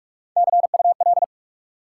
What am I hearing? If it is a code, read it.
Morse code, heard as a keyed tone, about 46 wpm, 700 Hz. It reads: CUL
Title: Top 90 QSO Elements 45wpm